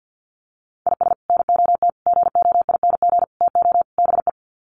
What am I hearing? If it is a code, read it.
Morse code, heard as a keyed tone, about 50 wpm, 700 Hz. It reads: IS NOT GOING TO BE
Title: Sets of 5 Words Encore2 50wpm